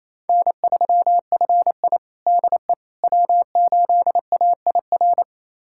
Morse code, heard as a keyed tone, about 28 wpm, 700 Hz. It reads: N3FI DE W8AIR
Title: USA Callsign DE Callsign 28wpm